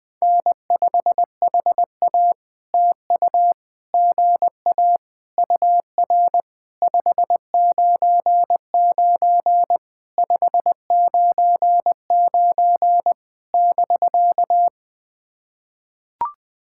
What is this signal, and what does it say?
Morse code, heard as a keyed tone, about 19 wpm, 700 Hz. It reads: N5HA TU GA UR 599 599 <BK>
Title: POTA QSO Practice - 20wpm